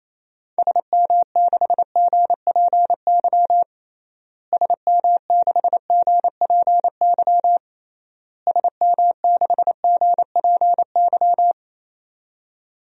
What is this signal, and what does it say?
Morse code, heard as a keyed tone, about 27 wpm, 700 Hz. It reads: SM6GPY SM6GPY SM6GPY
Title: DXCC-100 Call Signs - 6 Characters - Repeated 3x - 28wpm